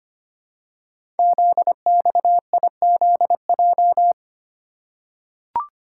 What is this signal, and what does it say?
Morse code, heard as a keyed tone, about 25 wpm, 700 Hz. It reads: ZXIZJ
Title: Code Groups - 5 Letters - 25wpm